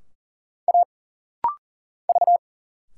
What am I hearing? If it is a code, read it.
Morse code, heard as a keyed tone, about 40 wpm, 700 Hz. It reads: A V
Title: Single Letters - Rapid-Fire 40wpm